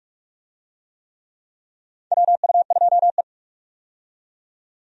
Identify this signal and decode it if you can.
Morse code, heard as a keyed tone, about 46 wpm, 700 Hz. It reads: WU2E
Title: USA Call Signs - 4 Characters - 45wpm